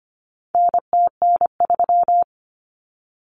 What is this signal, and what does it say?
Morse code, heard as a keyed tone, about 25 wpm, 700 Hz. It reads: NTN3
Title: Common Cut Numbers - 4-Digit 25wpm